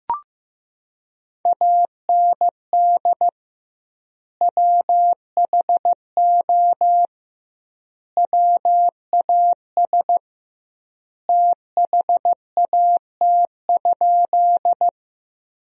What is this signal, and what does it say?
Morse code, heard as a keyed tone, about 15 wpm, 700 Hz. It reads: AND WHO WAS THAT?
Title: 2x Word-Spacing - Sentences from Top 100 Words - 15wpm